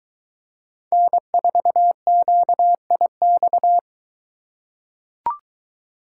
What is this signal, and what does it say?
Morse code, heard as a keyed tone, about 22 wpm, 700 Hz. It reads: N4QIX